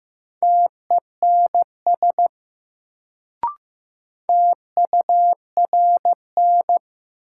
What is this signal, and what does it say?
Morse code, heard as a keyed tone, about 15 wpm, 700 Hz. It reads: TENS TURN
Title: Sets of 4 Words - New Word Drill - 15wpm